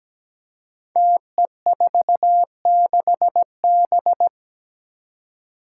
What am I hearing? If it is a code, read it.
Morse code, heard as a keyed tone, about 17 wpm, 700 Hz. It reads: TE46B
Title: Five Challenging Letter-Number Combinations 17wpm